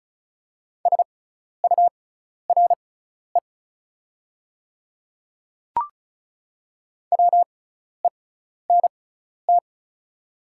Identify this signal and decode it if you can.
Morse code, heard as a keyed tone, about 36 wpm, spaced as for 12 wpm, 690 Hz. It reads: SURE WENT
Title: 6x Character Spacing - Top 300 Words - 35wpm